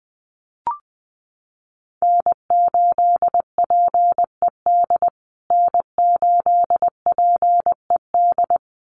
Morse code, heard as a keyed tone, about 20 wpm, 700 Hz. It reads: N8PED N8PED
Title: USA Call Signs Repeated 2x 20wpm